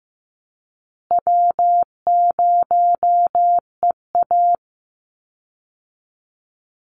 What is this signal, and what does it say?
Morse code, heard as a keyed tone, about 15 wpm, 700 Hz. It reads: W0EA
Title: Easy Call Signs Repeated 3x - 15wpm